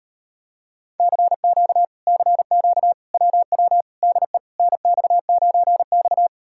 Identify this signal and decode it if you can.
Morse code, heard as a keyed tone, about 38 wpm, 700 Hz. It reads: CQ CQ WW DE NX9X